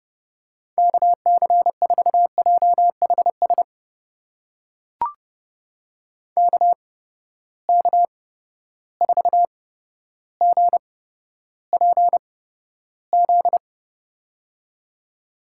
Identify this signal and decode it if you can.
Morse code, heard as a keyed tone, about 30 wpm, spaced as for 13 wpm, 700 Hz. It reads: KC4JHS KK4GPZ
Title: USA Call Signs - 6 Characters - 8x Character Spacing - 30wpm